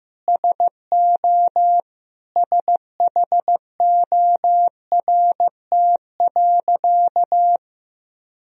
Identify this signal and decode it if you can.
Morse code, heard as a keyed tone, about 15 wpm, 700 Hz. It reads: SO SHORT.
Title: Sentences from Top 500 Words and Abbreviations 15wpm